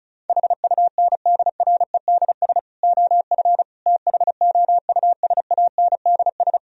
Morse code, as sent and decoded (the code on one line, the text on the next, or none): .... ..- -. -.. .-. -.. ... / --- ..-. / - .... --- ..- ... .- -. -.. ...
HUNDREDS OF THOUSANDS